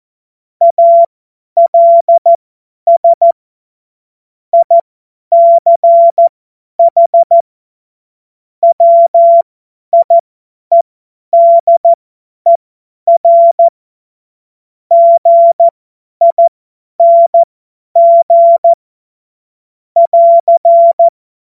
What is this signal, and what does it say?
Morse code, heard as a keyed tone, about 14 wpm, spaced as for 10 wpm, 680 Hz. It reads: ALS ICH WIEDER GING <AR>
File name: Morsekurs / Witze / Witz 05. Oktober 2021, 50 F70